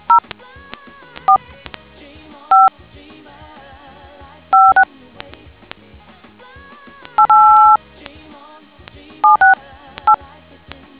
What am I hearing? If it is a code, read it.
DTMF, heard as telephone touch tones, tones of about 140 ms. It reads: 0 4 5 56 8 768